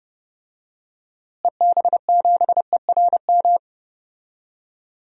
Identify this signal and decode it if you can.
Morse code, heard as a keyed tone, about 30 wpm, 700 Hz. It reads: EB7ERM
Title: DXCC-100 Call Signs - 6 Characters - Repeated 2x - 30wpm